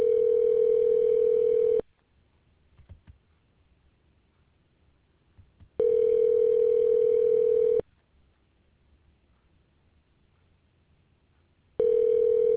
Ringback tone (North America), 440 + 480 Hz.